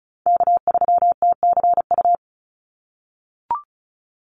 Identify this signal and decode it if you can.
Morse code, heard as a keyed tone, about 34 wpm, 700 Hz. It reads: K3TCU
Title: NA SOTA Chaser Callsigns 35wpm